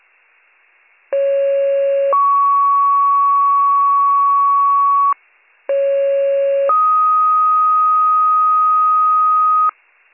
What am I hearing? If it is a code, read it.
Two-tone page, heard as a one-second tone, then a three-second tone, about 550 then 1120 Hz, about 550 then 1250 Hz. It is heard from 1.1 to 9.7 s.